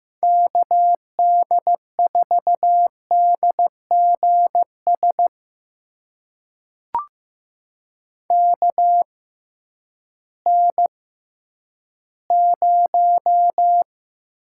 Morse code, heard as a keyed tone, about 15 wpm, spaced as for 10 wpm, 700 Hz. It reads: KD4DGS KN0
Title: USA Call Signs - 6 Characters - 6x Character Spacing - 15wpm